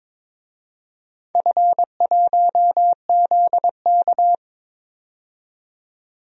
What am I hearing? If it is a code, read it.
Morse code, heard as a keyed tone, about 22 wpm, 700 Hz. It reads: F1ZK
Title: DXCC-100 Call Signs - 4 Characters - 22wpm